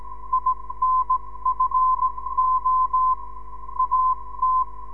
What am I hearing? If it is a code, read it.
Morse code, heard as a keyed tone, about 17 wpm, spaced as for 15 wpm, 1030 Hz. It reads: INFO AT
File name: ndb zvok / sm6bgp